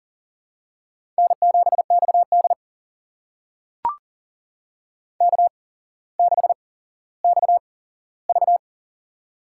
Morse code, heard as a keyed tone, about 37 wpm, spaced as for 17 wpm, 700 Hz. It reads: N7XD K6XV
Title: USA Call Signs - 4 Characters - 8x Character Spacing - 40wpm